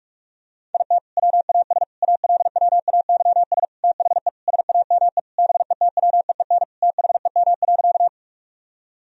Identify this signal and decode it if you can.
Morse code, heard as a keyed tone, about 46 wpm, 700 Hz. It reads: IT WAS ALWAYS THE SAME BETWEEN THEM.
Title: Sentences from Top 1000 Words - Review Part 1 - 45wpm